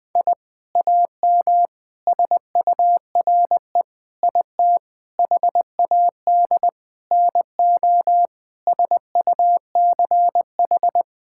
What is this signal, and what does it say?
Morse code, heard as a keyed tone, about 20 wpm, 700 Hz. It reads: I AM SURE IT HAD NO SUCH
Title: Sentences from Top 1000 Words - Review Part 6 - 20wpm